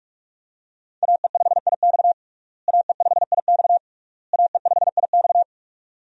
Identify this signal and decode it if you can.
Morse code, heard as a keyed tone, about 41 wpm, 700 Hz. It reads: AE5IX AE5IX AE5IX
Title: USA Call Signs - 5 Characters - Repeated 3x - 45wpm